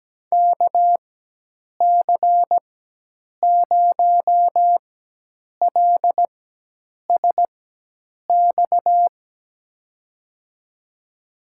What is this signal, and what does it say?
Morse code, heard as a keyed tone, about 17 wpm, spaced as for 8 wpm, 700 Hz. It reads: KC0LSX
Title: USA Call Signs - 4x Character Spacing - 17wpm